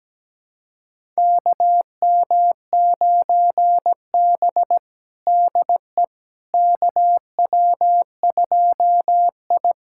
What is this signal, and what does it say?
Morse code, heard as a keyed tone, about 17 wpm, 700 Hz. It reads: KM9B DE KW2I
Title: USA Callsign DE Callsign - 4 Characters - 17wpm